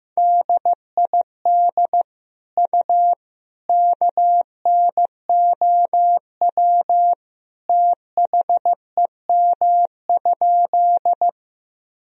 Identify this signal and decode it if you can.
Morse code, heard as a keyed tone, about 15 wpm, 700 Hz. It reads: DID U KNOW THEM?
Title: Sentences from Top 100 Words and Abbreviations 15wpm